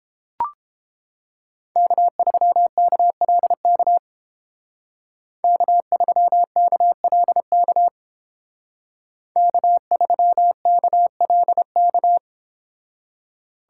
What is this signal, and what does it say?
Morse code, heard as a keyed tone, about 30 wpm, 700 Hz. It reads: K3KLK K3KLK K3KLK